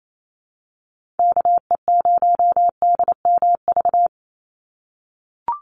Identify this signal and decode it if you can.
Morse code, heard as a keyed tone, about 28 wpm, 700 Hz. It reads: KE0DMV
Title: USA Call Signs 28wpm